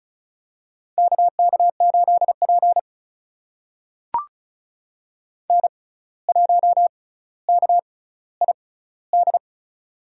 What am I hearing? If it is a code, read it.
Morse code, heard as a keyed tone, about 36 wpm, spaced as for 17 wpm, 700 Hz. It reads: KK8P N1KID
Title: USA Call Signs - 6x Character Spacing - 35wpm